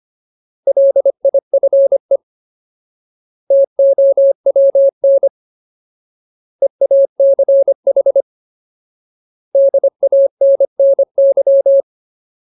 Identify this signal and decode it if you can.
Morse code, heard as a keyed tone, about 25 wpm, 550 Hz. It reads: LIFE TOWN EACH DANNY
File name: Ultimate Word List / 25WPM Ultimate